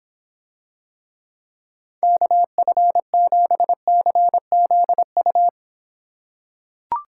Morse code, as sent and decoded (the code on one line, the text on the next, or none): -.- ..-. --... -.-. --.. ..-
KF7CZU